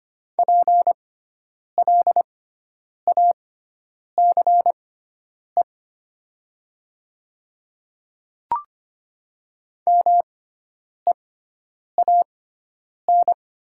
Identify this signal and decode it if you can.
Morse code, heard as a keyed tone, about 25 wpm, spaced as for 9 wpm, 700 Hz. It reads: PLACE MEAN
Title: 6x Character Spacing - Top 300 Words - 25wpm